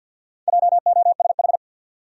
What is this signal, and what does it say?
Morse code, heard as a keyed tone, about 49 wpm, 700 Hz. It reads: JOSH
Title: USA Common Names - 50wpm